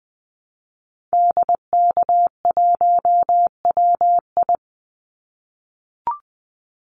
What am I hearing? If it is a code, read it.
Morse code, heard as a keyed tone, about 20 wpm, 700 Hz. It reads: DK1WI
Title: Call Signs - 20wpm